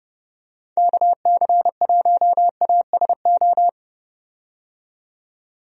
Morse code, heard as a keyed tone, about 30 wpm, 700 Hz. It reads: KC1ASO
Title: USA Call Signs - 6 Characters - 30wpm